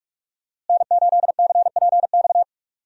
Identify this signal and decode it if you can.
Morse code, heard as a keyed tone, about 46 wpm, 700 Hz. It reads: N8CPX
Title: USA Call Signs - 5 Characters - Repeated 3x - 45wpm